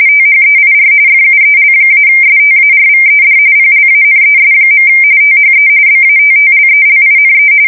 RTTY, about 45.45 baud. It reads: THEIR FUEL-STARVED PLANES. THE BATTLELINE
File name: fsk170 / USS-New-Jersey